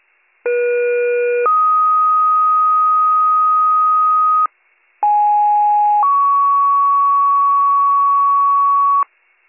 Two-tone page, heard as a one-second tone, then a three-second tone, about 480 then 1250 Hz, about 830 then 1120 Hz. It is heard from 0.4 to 9.0 s.